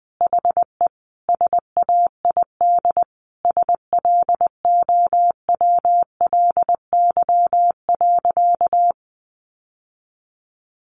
Morse code, heard as a keyed tone, about 20 wpm, 700 Hz. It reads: HE SAID SLOWLY.
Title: Sentences from Top 1100 Words 20wpm